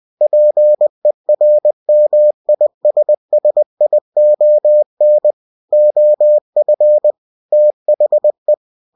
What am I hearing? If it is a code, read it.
Morse code, heard as a keyed tone, about 20 wpm, 600 Hz. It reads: PERMISSION OF THE
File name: cw-books / winnie-the-pooh / winnie-20wpm-page-0115